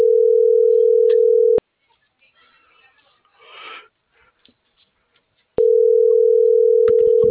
Ringback tone (North America), 440 + 480 Hz.